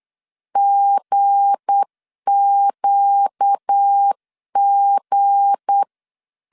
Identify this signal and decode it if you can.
Morse code, heard as a keyed tone, about 8 wpm, 800 Hz. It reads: GQG
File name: CW / DET / 1 9 1500C S B 8wpm 5624